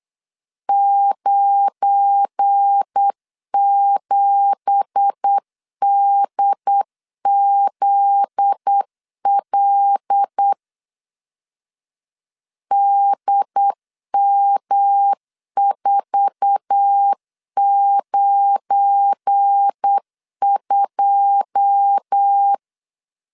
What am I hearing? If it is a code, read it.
Morse code, heard as a keyed tone, about 8 wpm, 800 Hz. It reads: 97DZL DM492
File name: LEZIONI / 10 SIGNS / 10A SLASH 8wpm 3452